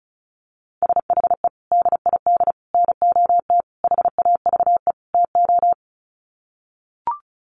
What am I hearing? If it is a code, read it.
Morse code, heard as a keyed tone, about 34 wpm, 700 Hz. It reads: SHE DID NOT HAVE TO
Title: Sets of 5 Words Encore2 35wpm